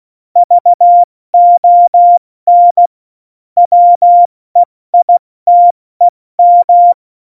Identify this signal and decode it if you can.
Morse code, heard as a keyed tone, about 16 wpm, 700 Hz. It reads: VON WEITEM
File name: Morsekurs / Witze / Witz 05. Januar 2021, 70 F80